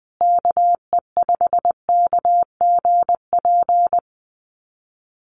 Morse code, heard as a keyed tone, about 19 wpm, 700 Hz. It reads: KE5KGP
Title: USA Call Signs Repeated 3x 20wpm